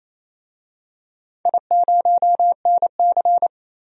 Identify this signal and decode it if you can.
Morse code, heard as a keyed tone, about 28 wpm, 700 Hz. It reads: I0NC
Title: DXCC-100 Call Signs - 4 Characters - 28wpm